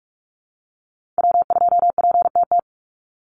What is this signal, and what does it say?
Morse code, heard as a keyed tone, about 44 wpm, 700 Hz. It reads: W2PTT
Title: Call Signs Repeated 2x - 45wpm